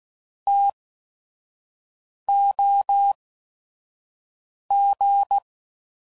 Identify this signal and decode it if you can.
Morse code, heard as a keyed tone, about 16 wpm, spaced as for 5 wpm, 800 Hz. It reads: TOG